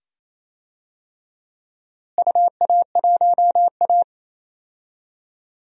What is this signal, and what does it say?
Morse code, heard as a keyed tone, about 28 wpm, 700 Hz. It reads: UA1A
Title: DXCC-100 Call Signs - 4 Characters - Repeated 2x - 28wpm